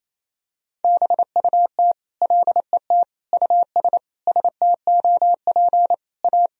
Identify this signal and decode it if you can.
Morse code, heard as a keyed tone, about 28 wpm, 700 Hz. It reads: BUT LET US STOP A
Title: Sentences from Top 1000 Words - Review Part 1 - 28wpm